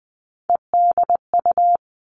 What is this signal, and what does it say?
Morse code, heard as a keyed tone, about 20 wpm, 700 Hz. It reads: EDU
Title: All Cut Numbers - 3-Digit 20wpm